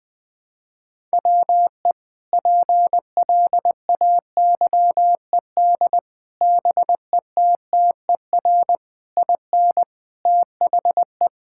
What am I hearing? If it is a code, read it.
Morse code, heard as a keyed tone, about 20 wpm, 700 Hz. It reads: WE PLAYED BETTER IN THE
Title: Sentences with 5-Word Phrases - 20wpm